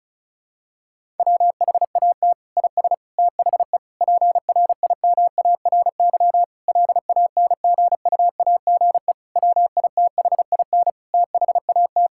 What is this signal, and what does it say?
Morse code, heard as a keyed tone, about 36 wpm, 700 Hz. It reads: WHAT IS THE PRIMARY LANGUAGE WITHIN THAT
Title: Sentences from Top 1000 Words - Review Part 5 - 35wpm